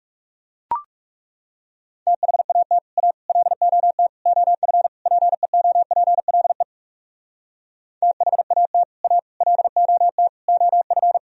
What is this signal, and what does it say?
Morse code, heard as a keyed tone, about 38 wpm, 700 Hz. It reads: THAT A LOT OF PEOPLE THAT A LOT OF